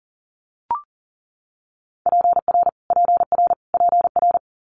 Morse code, heard as a keyed tone, about 41 wpm, 700 Hz. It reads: PR PR PR
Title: Multi-letter Phonemes Repeated 3x 40wpm